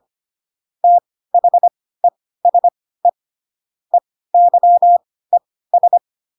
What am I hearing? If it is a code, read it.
Morse code, heard as a keyed tone, about 25 wpm, spaced as for 16 wpm, 700 Hz. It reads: THESE EYES
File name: Pinocchio / Pinocchio ch3 10 25 ik1hge